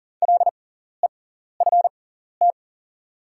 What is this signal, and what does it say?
Morse code, heard as a keyed tone, about 40 wpm, spaced as for 14 wpm, 690 Hz. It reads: LEFT
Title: 6x Character Spacing - Top 500 Words - 40wpm